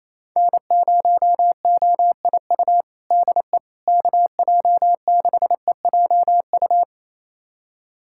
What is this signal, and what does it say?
Morse code, heard as a keyed tone, about 27 wpm, 700 Hz. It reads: N0OIU DE KJ6EJU
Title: USA Callsign DE Callsign 28wpm